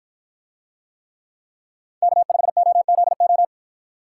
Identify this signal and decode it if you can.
Morse code, heard as a keyed tone, about 48 wpm, 700 Hz. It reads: K5OZO